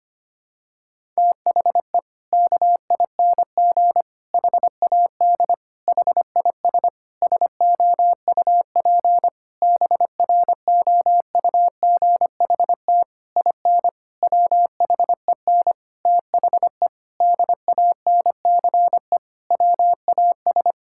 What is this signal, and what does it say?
Morse code, heard as a keyed tone, about 25 wpm, 700 Hz. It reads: THE KING HAD HIS SOUP BROUGHT IN WHEN THE DANCE WAS
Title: Sentences from Top 1000 Words - Review Part 6 - 25wpm